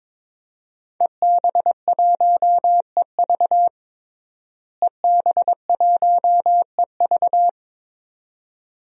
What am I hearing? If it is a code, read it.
Morse code, heard as a keyed tone, about 22 wpm, 700 Hz. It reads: EB1EV EB1EV
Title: DXCC-100 Call Signs - 5 Characters - Repeated 2x - 22wpm